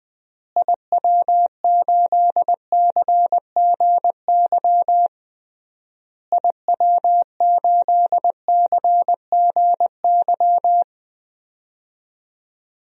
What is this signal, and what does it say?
Morse code, heard as a keyed tone, about 20 wpm, 700 Hz. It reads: IW8CGY IW8CGY
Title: DXCC-100 Call Signs - 6 Characters - Repeated 3x - 20wpm